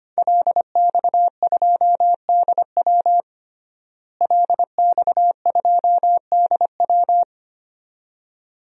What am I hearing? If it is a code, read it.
Morse code, heard as a keyed tone, about 25 wpm, 700 Hz. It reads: LX2DW LX2DW
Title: DXCC-100 Call Signs - 5 Characters - Repeated 2x - 25wpm